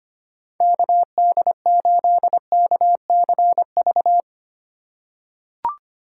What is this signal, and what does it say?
Morse code, heard as a keyed tone, about 25 wpm, 700 Hz. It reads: KD8KCV
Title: USA Call Signs - 6 Characters - 2x Character Spacing - 25wpm